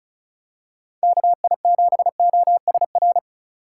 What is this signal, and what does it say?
Morse code, heard as a keyed tone, about 34 wpm, 700 Hz. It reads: KI7OSR
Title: USA Call Signs - 6 Characters - 3x Character Spacing - 35wpm